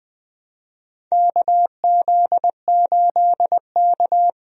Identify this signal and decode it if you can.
Morse code, heard as a keyed tone, about 20 wpm, 700 Hz. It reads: KZ8K
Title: USA Call Signs - 4 Characters - Repeated 3x - 20wpm